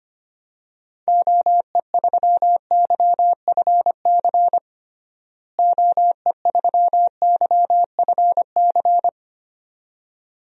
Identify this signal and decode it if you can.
Morse code, heard as a keyed tone, about 25 wpm, 700 Hz. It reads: OE3YFC OE3YFC
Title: DXCC-100 Call Signs - 6 Characters - Repeated 2x - 25wpm